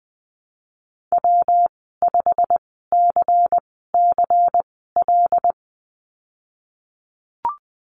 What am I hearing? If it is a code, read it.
Morse code, heard as a keyed tone, about 20 wpm, spaced as for 15 wpm, 700 Hz. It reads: W5CCL
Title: 128 Lesson 21g - Number 6 - Call Signs 20wpm